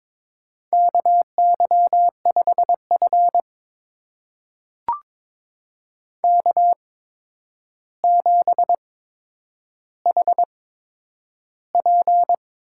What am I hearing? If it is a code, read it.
Morse code, heard as a keyed tone, about 22 wpm, spaced as for 9 wpm, 700 Hz. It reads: KY5F K7HP
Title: USA Call Signs - 4 Characters - 8x Character Spacing - 22wpm